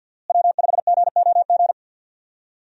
Morse code, heard as a keyed tone, about 49 wpm, 700 Hz. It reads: W5GOG
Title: USA Call Signs - 5 Characters - Repeated 2x - 50wpm